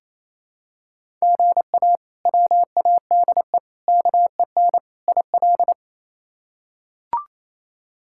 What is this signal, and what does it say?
Morse code, heard as a keyed tone, about 28 wpm, 700 Hz. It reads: GA WADE KEN IL
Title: K1USN SST QSO Practice - 28wpm